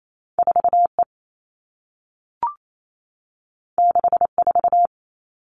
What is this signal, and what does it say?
Morse code, heard as a keyed tone, about 28 wpm, 700 Hz. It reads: 4E 64
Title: All Cut Numbers - 2-Digit 28wpm